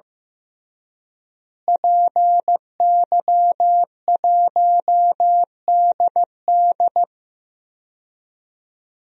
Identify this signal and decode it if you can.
Morse code, heard as a keyed tone, about 15 wpm, 700 Hz. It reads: PY1DD
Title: DXCC-100 Call Signs - 5 Characters - Repeated 3x - 15wpm